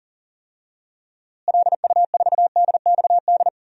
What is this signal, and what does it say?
Morse code, heard as a keyed tone, about 40 wpm, 700 Hz. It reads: LU4DXD
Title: DXCC-100 Call Signs - 6 Characters - Repeated 3x - 40wpm